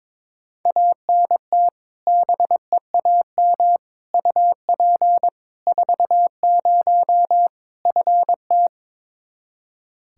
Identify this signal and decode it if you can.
Morse code, heard as a keyed tone, about 21 wpm, 700 Hz. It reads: ANT BEAM UP 40 FT
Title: Practice QSO - Exchange 2 - Info - 22wpm